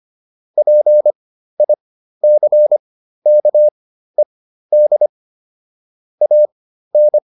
Morse code, heard as a keyed tone, about 25 wpm, spaced as for 13 wpm, 600 Hz. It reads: PICKED AN